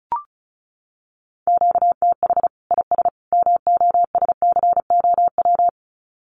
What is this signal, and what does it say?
Morse code, heard as a keyed tone, about 34 wpm, 700 Hz. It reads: QTH IS MOSCOW
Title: Common QSO Phrases 35wpm